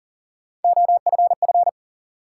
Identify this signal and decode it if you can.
Morse code, heard as a keyed tone, about 40 wpm, 700 Hz. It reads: OFF
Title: Sentences from Top 200 Words - New Word Drill - 40wpm